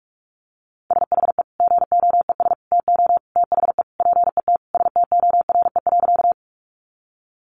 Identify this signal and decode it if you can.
Morse code, heard as a keyed tone, about 44 wpm, 700 Hz. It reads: SHE GOES TO THE PET STORE.
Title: Easy Phrases 45wpm